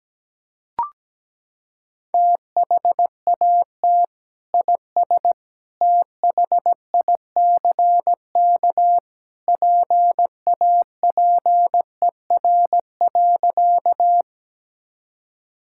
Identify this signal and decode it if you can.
Morse code, heard as a keyed tone, about 17 wpm, 700 Hz. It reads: THAT IS THICK PAPER.
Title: Sentences from Top 1600 Words 17wpm